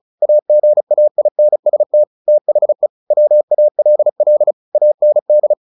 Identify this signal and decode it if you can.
Morse code, heard as a keyed tone, about 34 wpm, 600 Hz. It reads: AGAINST THE WALL AND
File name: the old man and the sea 2 35wpm